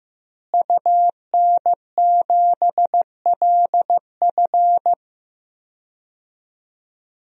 Morse code, heard as a keyed tone, about 15 wpm, 700 Hz. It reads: UN7LF
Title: DXCC-100 Call Signs - 5 Characters - Repeated 3x - 15wpm